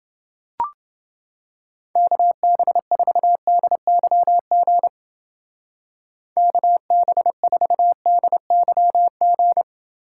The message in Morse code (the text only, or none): KB4DYG KB4DYG